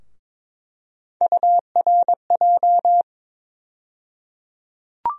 Morse code, heard as a keyed tone, about 21 wpm, 700 Hz. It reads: URJ
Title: USA Call Sign Postfixes - 3 Letters 22wpm